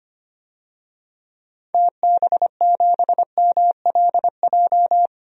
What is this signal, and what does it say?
Morse code, heard as a keyed tone, about 25 wpm, 690 Hz. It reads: TB7MLJ
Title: DXCC-100 Call Signs - 6 Characters - 25wpm